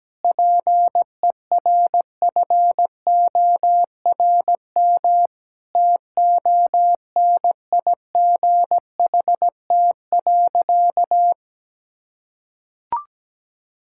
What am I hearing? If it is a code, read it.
Morse code, heard as a keyed tone, about 17 wpm, 700 Hz. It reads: PERFORM TONIGHT.
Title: Sentences from Top 1000 Words - Review Part 5 - 17wpm